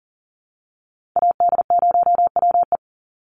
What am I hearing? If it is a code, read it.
Morse code, heard as a keyed tone, about 40 wpm, 700 Hz. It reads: AD0WE
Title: Call Signs Repeated 3x - 40wpm